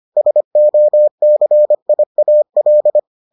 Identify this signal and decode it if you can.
Morse code, heard as a keyed tone, about 25 wpm, 600 Hz. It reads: SOCIAL